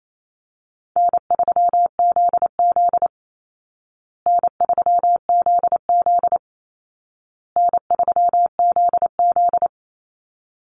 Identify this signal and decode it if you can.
Morse code, heard as a keyed tone, about 28 wpm, 700 Hz. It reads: N3ZZ N3ZZ N3ZZ
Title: Call Signs Repeated 2x - 28wpm